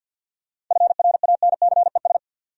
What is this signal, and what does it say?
Morse code, heard as a keyed tone, about 52 wpm, 700 Hz. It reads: FRANCES
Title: USA Common Names - 50wpm